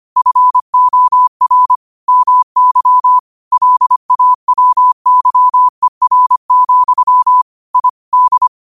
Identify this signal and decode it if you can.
Morse code, heard as a keyed tone, about 25 wpm, 1000 Hz. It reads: FOR MY LAWYER, I D
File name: qotdfiles / 25wpm